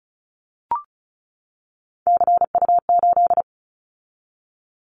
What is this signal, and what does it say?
Morse code, heard as a keyed tone, about 36 wpm, 700 Hz. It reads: CU8
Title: Callsign Prefixes for Top 50 DXCC - 35wpm